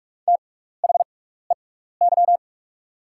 Morse code, heard as a keyed tone, about 44 wpm, spaced as for 15 wpm, 700 Hz. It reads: THEY
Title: 6x Character Spacing - Top 200 Words - 45wpm